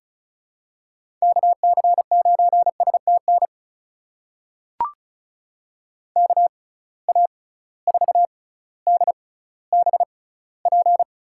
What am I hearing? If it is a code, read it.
Morse code, heard as a keyed tone, about 34 wpm, spaced as for 18 wpm, 700 Hz. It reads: KC9STN KA4DBP
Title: USA Call Signs - 6x Character Spacing - 35wpm